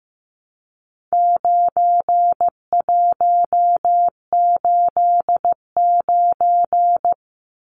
Morse code, heard as a keyed tone, about 15 wpm, 700 Hz. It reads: 9189
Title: Numbers - 4-Digit 15wpm